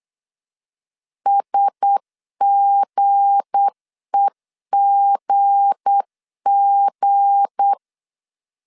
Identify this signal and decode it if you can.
Morse code, heard as a keyed tone, about 8 wpm, 800 Hz. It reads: SGEGG
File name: LEZIONI / 7 GF9Z / 7 G 8wpm 3118